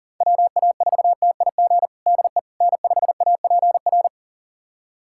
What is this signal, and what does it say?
Morse code, heard as a keyed tone, about 40 wpm, 700 Hz. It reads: WA4TIG DE N5APR
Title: USA Callsign DE Callsign 40wpm